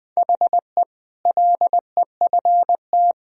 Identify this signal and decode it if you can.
Morse code, heard as a keyed tone, about 20 wpm, 700 Hz. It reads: HE LEFT
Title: Sentences from Top 1000 Words - Review Part 4 - 20wpm